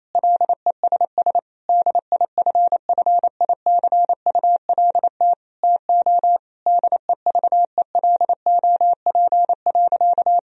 Morse code, heard as a keyed tone, about 28 wpm, 700 Hz. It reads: LESS DIFFICULT TO DEVELOP.
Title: Sentences from Top 1000 Words - Review Part 2 - 28wpm